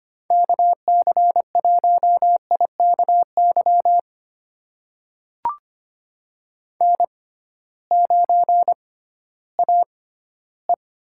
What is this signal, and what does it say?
Morse code, heard as a keyed tone, about 25 wpm, spaced as for 15 wpm, 700 Hz. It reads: KC1IKY N9AE
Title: USA Call Signs - 6x Character Spacing - 25wpm